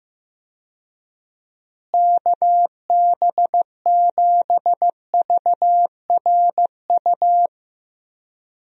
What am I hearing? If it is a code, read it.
Morse code, heard as a keyed tone, about 16 wpm, 700 Hz. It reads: KB7VRU
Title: USA Call Signs - 6 Characters - Repeated 2x - 15wpm